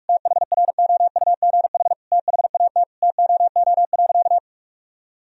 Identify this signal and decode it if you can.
Morse code, heard as a keyed tone, about 47 wpm, 700 Hz. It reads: THROUGH THAT TOO.
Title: Sentences from Top 1000 Words - Review Part 1 - 45wpm